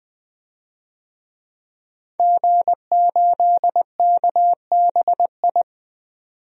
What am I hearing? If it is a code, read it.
Morse code, heard as a keyed tone, about 21 wpm, 700 Hz. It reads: G8KBI
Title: DXCC-100 Call Signs - 5 Characters - 20wpm